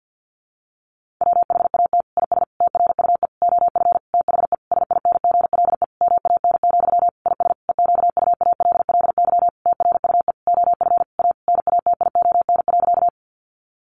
Morse code, heard as a keyed tone, about 50 wpm, 700 Hz. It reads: WHAT IS TRUE OF THE SINGLE MAN, IS EQUALLY TRUE OF A NATION.
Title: Sentences from Top 1400 Words 50wpm